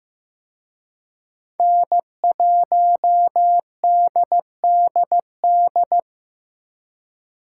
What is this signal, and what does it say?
Morse code, heard as a keyed tone, about 18 wpm, 700 Hz. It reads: N1DDD
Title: Speed Racing - USA 1x3 Call Signs 15wpm